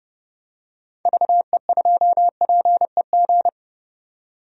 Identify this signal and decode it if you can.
Morse code, heard as a keyed tone, about 30 wpm, 700 Hz. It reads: VE2PEG
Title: DXCC-100 Call Signs - 6 Characters - 30wpm